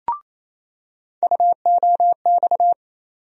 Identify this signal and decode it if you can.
Morse code, heard as a keyed tone, about 28 wpm, 700 Hz. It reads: UOX
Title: USA Call Sign Postfixes - 3 Letters 28wpm